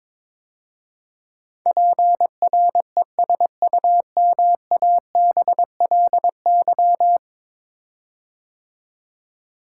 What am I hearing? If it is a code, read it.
Morse code, heard as a keyed tone, about 22 wpm, 700 Hz. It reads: PRESUMABLY
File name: practice / 22wpm / 10 Letter Words - First 3 Letters Given - 22wpm